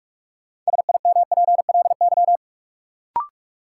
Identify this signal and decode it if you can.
Morse code, heard as a keyed tone, about 44 wpm, 700 Hz. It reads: SIMPLY
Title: Sentences from Top 600 Words - New Word Drill - 45wpm